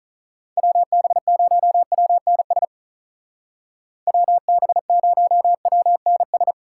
Morse code, heard as a keyed tone, about 43 wpm, 700 Hz. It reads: WB0WNS WB0WNS
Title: Speed Racing - USA 2x3 Call Signs 35wpm